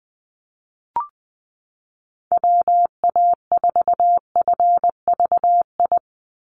Morse code, heard as a keyed tone, about 20 wpm, 700 Hz. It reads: WA4FVI
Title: USA Call Signs 20wpm